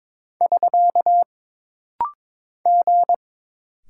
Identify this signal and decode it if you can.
Morse code, heard as a keyed tone, about 22 wpm, 700 Hz. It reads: <SK> G
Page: This rapid-fire timing is geared to push your ICR (Instant Character Recognition) skills.